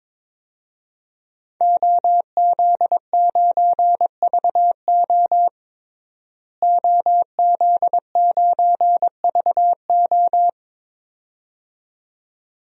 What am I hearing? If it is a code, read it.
Morse code, heard as a keyed tone, about 21 wpm, 700 Hz. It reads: OZ9VO OZ9VO
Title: DXCC-100 Call Signs - 5 Characters - Repeated 2x - 22wpm